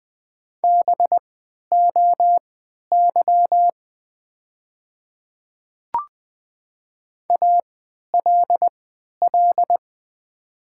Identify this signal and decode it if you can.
Morse code, heard as a keyed tone, about 20 wpm, spaced as for 11 wpm, 690 Hz. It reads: BOY ALL
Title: 3x Character Spacing - Top 300 Words - 20wpm